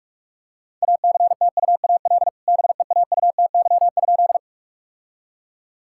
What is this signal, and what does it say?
Morse code, heard as a keyed tone, about 46 wpm, 700 Hz. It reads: ACTUAL BEAUTY?
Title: Sentences from Top 1000 Words - Review Part 5 - 45wpm